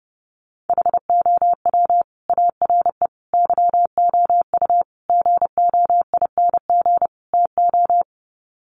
Morse code, heard as a keyed tone, about 30 wpm, 700 Hz. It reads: HOW ARE YOU GOING TO
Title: Sets of 5 Words 30wpm